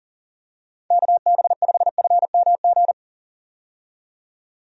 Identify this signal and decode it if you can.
Morse code, heard as a keyed tone, about 40 wpm, 700 Hz. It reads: KB5FMG
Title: USA Call Signs - 6 Characters - 40wpm